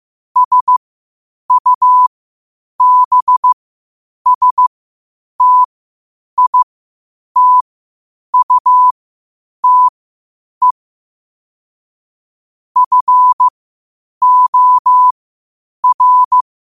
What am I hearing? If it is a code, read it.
Morse code, heard as a keyed tone, about 15 wpm, spaced as for 8 wpm, 1000 Hz. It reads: SUBSTITUTE FOR
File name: qotdfiles / 8wpm